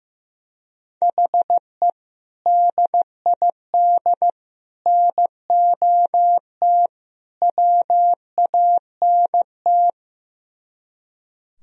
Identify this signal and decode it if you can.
Morse code, heard as a keyed tone, about 15 wpm, 700 Hz. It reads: HE DID NOT WANT
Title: Speed-Racing - Sets of 4 Words 15wpm